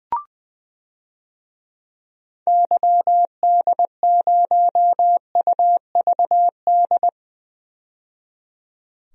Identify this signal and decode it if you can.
Morse code, heard as a keyed tone, about 20 wpm, 700 Hz. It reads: YD0UVD
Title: DXCC-100 Call Signs - 6 Characters - 20wpm